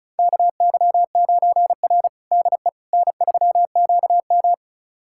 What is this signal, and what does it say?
Morse code, heard as a keyed tone, about 36 wpm, 700 Hz. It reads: KY9R DE N3QM
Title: USA Callsign DE Callsign - 4 Characters - 35wpm